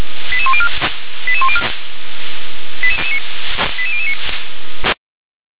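Five-tone selective call (ZVEI), heard as quick runs of five tones, tones of about 70 ms. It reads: 88104 88104 88009
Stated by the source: ZVEI five-tone selective call received on SDR.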